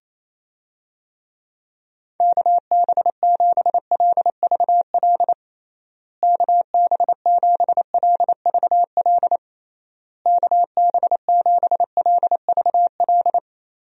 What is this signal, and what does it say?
Morse code, heard as a keyed tone, about 28 wpm, 700 Hz. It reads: KB7LVL KB7LVL KB7LVL
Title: USA Call Signs - 6 Characters - Repeated 3x - 28wpm